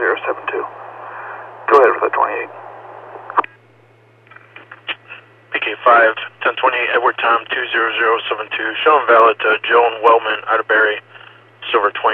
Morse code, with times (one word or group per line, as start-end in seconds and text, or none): none